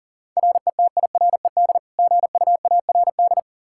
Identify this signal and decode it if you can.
Morse code, heard as a keyed tone, about 40 wpm, 700 Hz. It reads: RETIRED GUARD
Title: Practice QSO - Exchange 3 - More Info - 40wpm